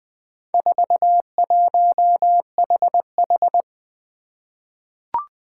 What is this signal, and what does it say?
Morse code, heard as a keyed tone, about 20 wpm, 700 Hz. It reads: 41HH
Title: Four Challenging Letter-Number Combinations 20wpm